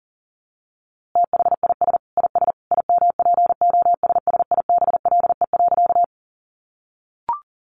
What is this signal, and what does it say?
Morse code, heard as a keyed tone, about 40 wpm, 700 Hz. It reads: THIS IS IMPOSSIBLE.
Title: A Sound Of Thunder 40wpm